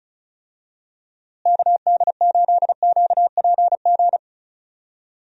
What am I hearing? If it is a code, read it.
Morse code, heard as a keyed tone, about 36 wpm, 700 Hz. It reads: KD8QPG
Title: USA Call Signs - 6 Characters - Repeated 3x - 35wpm